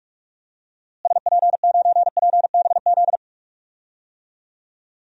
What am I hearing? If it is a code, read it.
Morse code, heard as a keyed tone, about 46 wpm, 700 Hz. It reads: SP9PBZ
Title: DXCC-100 Call Signs - 6 Characters - Repeated 2x - 45wpm